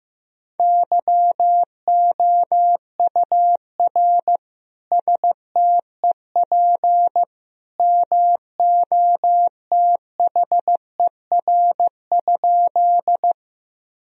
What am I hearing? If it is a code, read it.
Morse code, heard as a keyed tone, about 15 wpm, 700 Hz. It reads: YOUR STEP MOTHER?
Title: Sentences from Top 1000 Words - Review Part 2 - 15wpm